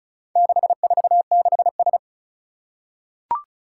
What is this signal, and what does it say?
Morse code, heard as a keyed tone, about 36 wpm, 700 Hz. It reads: 646S
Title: Four Challenging Letter-Number Combinations 35wpm